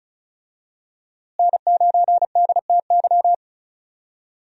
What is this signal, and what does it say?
Morse code, heard as a keyed tone, about 36 wpm, 700 Hz. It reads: N9DTY
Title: USA Call Signs - 5 Characters - Repeated 3x - 35wpm